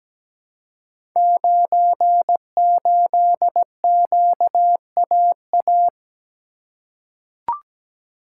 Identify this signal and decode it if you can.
Morse code, heard as a keyed tone, about 17 wpm, 700 Hz. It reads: 98QAA
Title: Code Groups - 5 Letter:Numbers - 17wpm